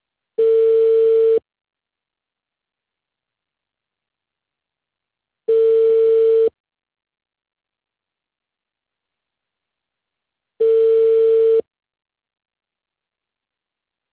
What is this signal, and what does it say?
Ringback tone (China, 450 Hz).